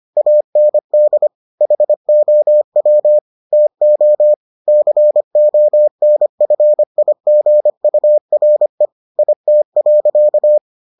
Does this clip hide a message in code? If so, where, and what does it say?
Yes: Morse code from 0.2 to 10.6 s, heard as a keyed tone, about 25 wpm, 600 Hz. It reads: AND HOW TO CONFIGURE IT.